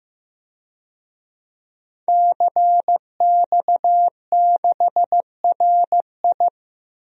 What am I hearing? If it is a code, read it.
Morse code, heard as a keyed tone, about 15 wpm, 700 Hz. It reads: CX6RI
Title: DXCC-100 Call Signs - 5 Characters - 15wpm